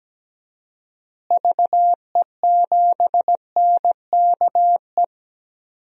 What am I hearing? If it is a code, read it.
Morse code, heard as a keyed tone, about 17 wpm, 700 Hz. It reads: VE7NKE
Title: DXCC-100 Call Signs - 6 Characters - Repeated 3x - 17wpm